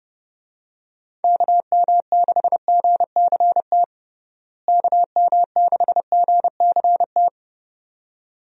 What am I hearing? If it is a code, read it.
Morse code, heard as a keyed tone, about 30 wpm, 700 Hz. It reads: KM6GCT KM6GCT
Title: USA Call Signs - 6 Characters - Repeated 2x - 30wpm